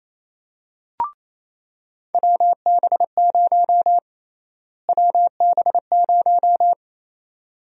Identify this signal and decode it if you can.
Morse code, heard as a keyed tone, about 28 wpm, 700 Hz. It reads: WB0 WB0
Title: USA Call Sign Prefixes - Repeated 2x - 28wpm